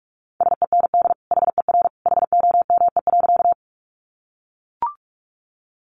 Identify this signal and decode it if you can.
Morse code, heard as a keyed tone, about 44 wpm, 700 Hz. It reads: SEND HER HOME.
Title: Sentences from Top 1500 Words 45wpm